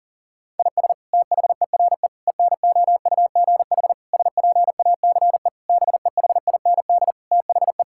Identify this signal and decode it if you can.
Morse code, heard as a keyed tone, about 40 wpm, 700 Hz. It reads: IS THERE ENOUGH SPACE BEHIND THE
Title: Sentences from Top 1000 Words - Review Part 5 - 40wpm